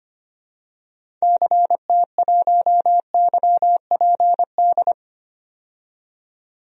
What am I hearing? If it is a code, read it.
Morse code, heard as a keyed tone, about 25 wpm, 700 Hz. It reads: CT1YPD
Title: DXCC-100 Call Signs - 6 Characters - 25wpm